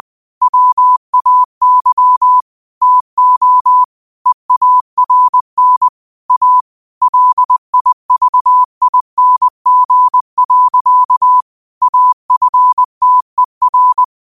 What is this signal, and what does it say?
Morse code, heard as a keyed tone, about 20 wpm, 1000 Hz. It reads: WAY TO EARN A LIVING. AFTER
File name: qotdfiles / 20wpm